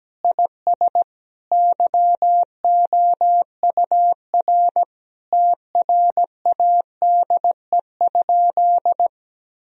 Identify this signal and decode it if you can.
Morse code, heard as a keyed tone, about 17 wpm, 700 Hz. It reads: IS YOUR TRADE?
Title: Sentences from Top 1000 Words - Review Part 4 - 17wpm